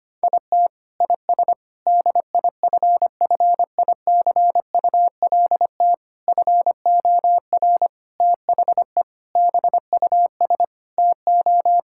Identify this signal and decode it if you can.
Morse code, heard as a keyed tone, about 25 wpm, 700 Hz. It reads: IT IS DIFFICULT FOR THE BUS TO
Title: Sentences from Top 1000 Words - Review Part 6 - 25wpm